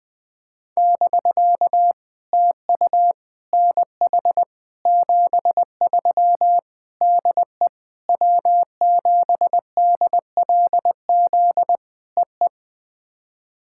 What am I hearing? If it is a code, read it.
Morse code, heard as a keyed tone, about 20 wpm, 700 Hz. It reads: <BK> TU NH 73 DE W7DLZ EE
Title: POTA QSO Practice - 20wpm